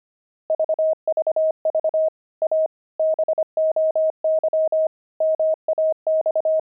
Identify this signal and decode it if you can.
Morse code, heard as a keyed tone, about 25 wpm, 630 Hz. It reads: VVV A BOY MAX
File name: pangram-fs-25-cs-25-weaving